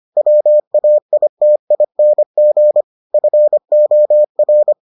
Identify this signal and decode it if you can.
Morse code, heard as a keyed tone, about 25 wpm, 600 Hz. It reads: WAITING FOR